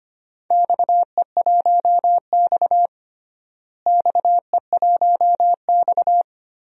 Morse code, heard as a keyed tone, about 25 wpm, 700 Hz. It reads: XE1X XE1X
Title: DXCC-100 Call Signs - 4 Characters - Repeated 3x - 25wpm